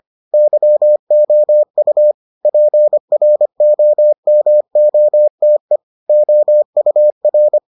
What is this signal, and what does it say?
Morse code, heard as a keyed tone, about 25 wpm, 600 Hz. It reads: YOU PROMOTE OUR